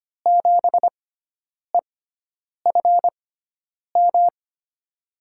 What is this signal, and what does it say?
Morse code, heard as a keyed tone, about 25 wpm, spaced as for 9 wpm, 700 Hz. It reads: 7EFM
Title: USA Call Signs - 6 Characters - 6x Character Spacing - 25wpm